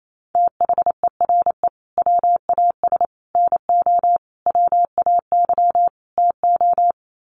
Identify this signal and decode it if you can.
Morse code, heard as a keyed tone, about 28 wpm, 700 Hz. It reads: THERE WAS NO WAY TO
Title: Sets of 5 Words Encore 28wpm